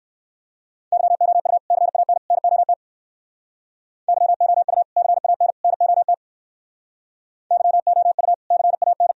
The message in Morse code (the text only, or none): YOU CAN NOT YOU CAN NOT YOU CAN